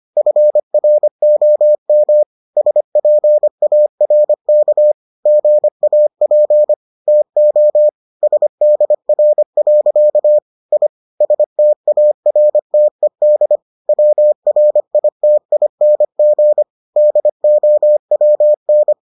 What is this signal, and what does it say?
Morse code, heard as a keyed tone, about 25 wpm, 600 Hz. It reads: FROM SPARK GAP TO SDR. I STARTED WRITING DOWN